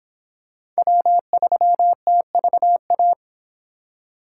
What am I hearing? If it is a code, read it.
Morse code, heard as a keyed tone, about 25 wpm, 700 Hz. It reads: W3TVA